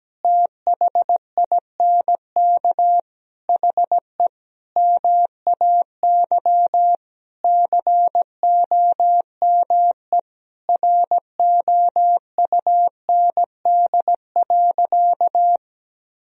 Morse code, heard as a keyed tone, about 17 wpm, 700 Hz. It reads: THINK HE MAY COME ROUND.
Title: Sentences from Top 1000 Words - Review Part 1 - 17wpm